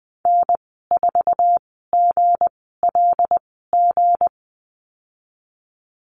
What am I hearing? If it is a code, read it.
Morse code, heard as a keyed tone, about 20 wpm, spaced as for 14 wpm, 700 Hz. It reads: N4GLG
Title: 161 Lesson 26g - Letter g - Call Signs 20wpm